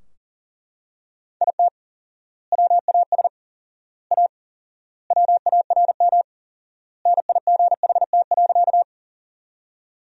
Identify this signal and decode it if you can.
Morse code, heard as a keyed tone, about 40 wpm, 700 Hz. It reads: IT WAS A WARM NIGHT.
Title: 4x Word-Spacing - Sentences from Top 500 Words - 40wpm